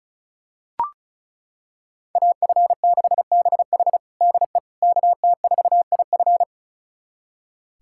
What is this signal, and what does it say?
Morse code, heard as a keyed tone, about 36 wpm, 700 Hz. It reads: AF6BH DE KT4IF
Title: USA Callsign DE Callsign - 5 Characters - 35wpm